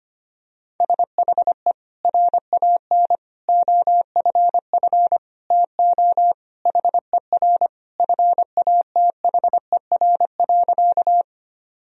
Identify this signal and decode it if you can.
Morse code, heard as a keyed tone, about 25 wpm, 700 Hz. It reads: SHE RAN OFF TO HER FATHER.
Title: Sentences from Top 1000 Words - Review Part 3 - 25wpm